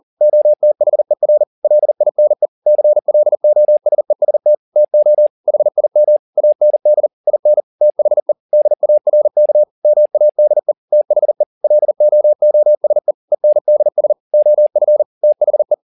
Morse code, heard as a keyed tone, about 40 wpm, 600 Hz. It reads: OTHER LINE CLOSEST TO HIM AND IN THE DARK MADE THE LOOSE ENDS OF THE